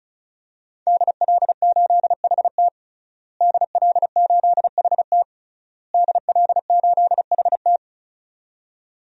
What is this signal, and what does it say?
Morse code, heard as a keyed tone, about 34 wpm, 700 Hz. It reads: DL8HT DL8HT DL8HT
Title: DXCC-100 Call Signs - 5 Characters - Repeated 3x - 35wpm